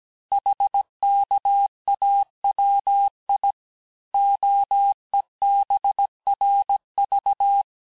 Morse code, heard as a keyed tone, about 17 wpm, 800 Hz. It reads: HKAWI OEBRV
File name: corrige-cours-1581-texte-chiffre-vitesse-16-mpm